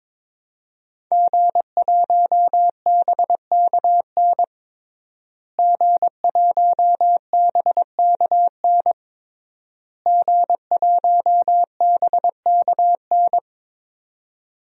Morse code, heard as a keyed tone, about 22 wpm, 700 Hz. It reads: G1BKN G1BKN G1BKN
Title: DXCC-100 Call Signs - 5 Characters - Repeated 3x - 22wpm